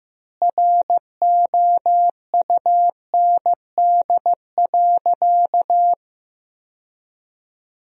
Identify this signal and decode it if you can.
Morse code, heard as a keyed tone, about 16 wpm, 700 Hz. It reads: ROUND.
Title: 4x Word-Spacing - Sentences from Top 200 Words - 15wpm